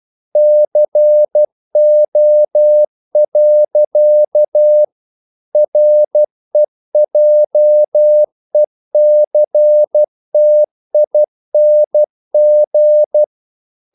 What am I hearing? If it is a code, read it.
Morse code, heard as a keyed tone, about 12 wpm, 600 Hz. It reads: CO. REJECTING
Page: English quotations in CW (telegraphy - morse)